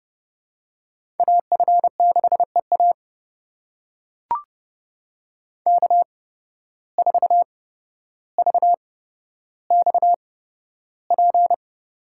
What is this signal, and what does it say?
Morse code, heard as a keyed tone, about 30 wpm, spaced as for 13 wpm, 700 Hz. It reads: AF6EA K4VXP
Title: USA Call Signs - 5 Characters - 8x Character Spacing - 30wpm